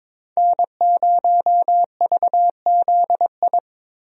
Morse code, heard as a keyed tone, about 22 wpm, 700 Hz. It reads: N0VZI
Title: USA Call Signs - 5 Characters - Repeated 2x - 22wpm